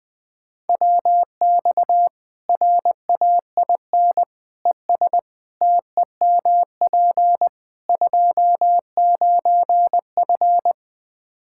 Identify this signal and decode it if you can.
Morse code, heard as a keyed tone, about 21 wpm, 700 Hz. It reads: WX RAIN ES TEMP 29F
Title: Practice QSO - Exchange 2 - Info - 20wpm